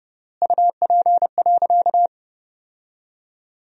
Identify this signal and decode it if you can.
Morse code, heard as a keyed tone, about 32 wpm, 700 Hz. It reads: UP.
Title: 3x Word-Spacing - Sentences from Top 500 Words - 30wpm